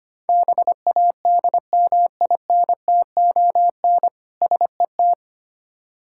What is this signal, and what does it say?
Morse code, heard as a keyed tone, about 25 wpm, 700 Hz. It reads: BADMINTON SET
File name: practice / 25wpm / Difficult Words with Hints - 25wpm